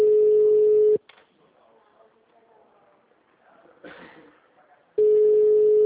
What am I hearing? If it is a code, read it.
Ringback tone (Europe, 425 Hz), with speech.
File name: out- -2001- - - .1934